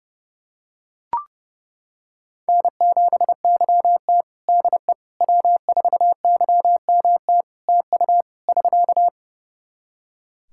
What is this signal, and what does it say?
Morse code, heard as a keyed tone, about 30 wpm, 700 Hz. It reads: N7YT DE W4YMT TU <SK>
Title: K1USN SST QSO Practice 30wpm